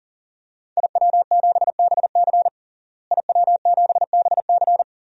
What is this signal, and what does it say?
Morse code, heard as a keyed tone, about 37 wpm, 700 Hz. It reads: IW7BC IW7BC
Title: DXCC-100 Call Signs - 5 Characters - Repeated 3x - 40wpm